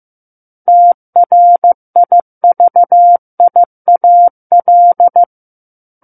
Morse code, heard as a keyed tone, about 15 wpm, 700 Hz. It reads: TRIVIAL
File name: Chapter 6 - 15 WPM